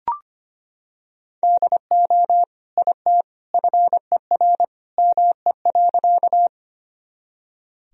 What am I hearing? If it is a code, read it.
Morse code, heard as a keyed tone, about 25 wpm, 700 Hz. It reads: DO IT FER ME.
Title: Sentences from Top 200 Words and Abbreviations 25wpm